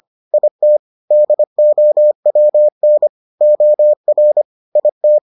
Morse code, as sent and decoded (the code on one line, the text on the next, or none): .. - / -.. --- .-- -. / --- .-. / .. -
IT DOWN OR IT